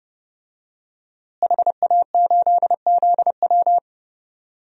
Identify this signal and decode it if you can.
Morse code, heard as a keyed tone, about 31 wpm, 700 Hz. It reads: HA8ZW
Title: DXCC-100 Call Signs - 5 Characters - Repeated 2x - 30wpm